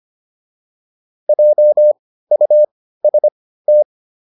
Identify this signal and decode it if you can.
Morse code, heard as a keyed tone, about 25 wpm, spaced as for 15 wpm, 600 Hz. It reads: JUST